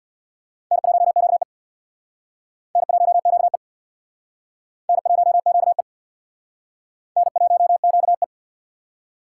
Morse code, heard as a keyed tone, about 60 wpm, 700 Hz. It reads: N1QE N1QE N1QE N1QE
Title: Speed Racing - USA 1x2 Call Signs 50wpm